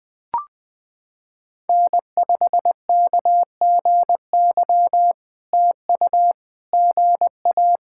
Morse code, heard as a keyed tone, about 20 wpm, 690 Hz. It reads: N5KGY TU GA
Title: POTA QSO Practice - 20wpm